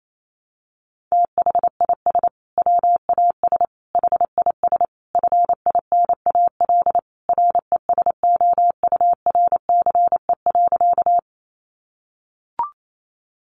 Morse code, heard as a keyed tone, about 28 wpm, 700 Hz. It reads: THIS WAS HIS FINAL RESOURCE.
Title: Sentences from Top 1300 Words 28wpm